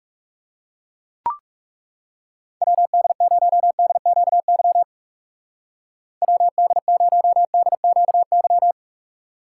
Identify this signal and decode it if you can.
Morse code, heard as a keyed tone, about 37 wpm, 700 Hz. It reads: WD0DQY WD0DQY
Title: Speed Racing - USA 2x3 Call Signs 30wpm